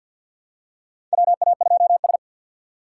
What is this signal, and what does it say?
Morse code, heard as a keyed tone, about 50 wpm, 700 Hz. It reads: WA2S
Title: USA Call Signs - 4 Characters - 3x Character Spacing - 50wpm